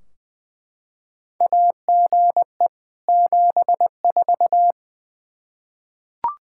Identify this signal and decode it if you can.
Morse code, heard as a keyed tone, about 21 wpm, 700 Hz. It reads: AGE 74
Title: Practice QSO - Exchange 3 - More Info - 20wpm